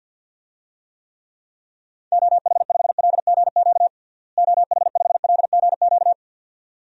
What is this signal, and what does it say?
Morse code, heard as a keyed tone, about 48 wpm, 700 Hz. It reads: OH5LGQ OH5LGQ
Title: DXCC-100 Call Signs - 6 Characters - Repeated 2x - 50wpm